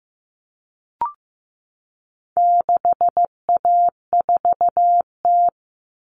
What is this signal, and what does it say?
Morse code, heard as a keyed tone, about 15 wpm, 700 Hz. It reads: 6A4T
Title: Common Cut Numbers - 4-Digit 15wpm